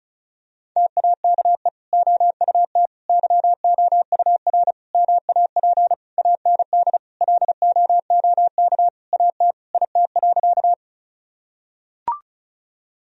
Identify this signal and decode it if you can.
Morse code, heard as a keyed tone, about 36 wpm, 700 Hz. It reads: TAKE OUT YOUR MAP AND LOOK AT IT.
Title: Sentences from Top 1000 Words - Review Part 2 - 35wpm